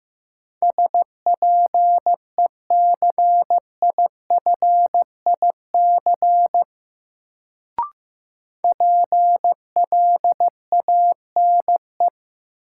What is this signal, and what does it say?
Morse code, heard as a keyed tone, about 15 wpm, 700 Hz. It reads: SPECIFIC PLANE
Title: Sentences from Top 1000 Words - Review Part 3 - New Word Drill - 15wpm